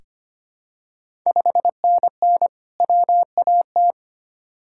Morse code, heard as a keyed tone, about 25 wpm, 700 Hz. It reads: 5NN WAT
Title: Ontario QSO Party Exchange 25wpm